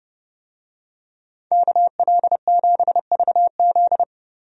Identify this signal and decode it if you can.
Morse code, heard as a keyed tone, about 32 wpm, 700 Hz. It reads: KL7VZ
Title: DXCC-100 Call Signs - 5 Characters - 30wpm